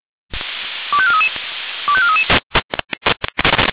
Five-tone selective call (ZVEI), heard as quick runs of five tones, tones of about 70 ms. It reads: 25433 25433